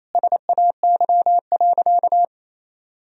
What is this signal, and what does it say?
Morse code, heard as a keyed tone, about 28 wpm, 700 Hz. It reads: SAY.
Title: Sentences from Top 1000 Words - Review Part 2 - 28wpm